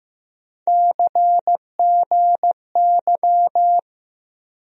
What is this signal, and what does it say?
Morse code, heard as a keyed tone, about 15 wpm, 700 Hz. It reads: CGY
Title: USA Call Sign Postfixes - 3 Letters 15wpm